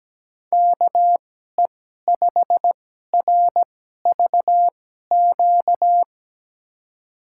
Morse code, heard as a keyed tone, about 17 wpm, spaced as for 12 wpm, 700 Hz. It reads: KE5RVQ
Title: USA Call Signs - 6 Characters - 2x Character Spacing - 17wpm